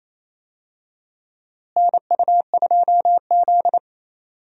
Morse code, heard as a keyed tone, about 28 wpm, 700 Hz. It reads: NU2Z
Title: USA Call Signs - 4 Characters - Repeated 3x - 28wpm